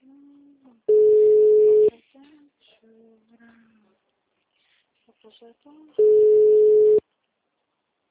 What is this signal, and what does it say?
Ringback tone (Europe, 425 Hz), with speech.